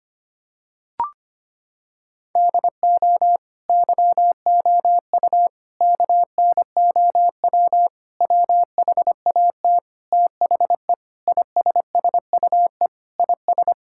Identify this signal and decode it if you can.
Morse code, heard as a keyed tone, about 25 wpm, 700 Hz. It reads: DO YOU KNOW WHAT THE ISSUE IS
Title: Sentences from Top 1000 Words - Review Part 6 - 25wpm